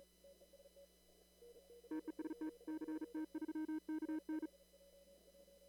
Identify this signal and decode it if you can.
Morse code, heard as a keyed tone, about 37 wpm, 310 Hz. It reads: TEST CT3KN
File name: CW / kt5j wpx cw-op-k5pi